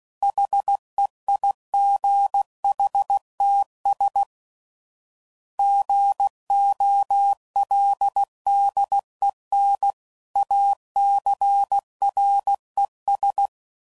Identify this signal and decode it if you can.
Morse code, heard as a keyed tone, about 16 wpm, 800 Hz. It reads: HEIGHTS GOLDEN ACRES
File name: CW Code Groups @15WPM / fromSM's / Cities